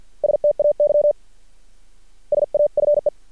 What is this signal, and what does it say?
Morse code, heard as a keyed tone, about 50 wpm, 590 Hz. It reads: STAY SAFE
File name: archived / CRSnet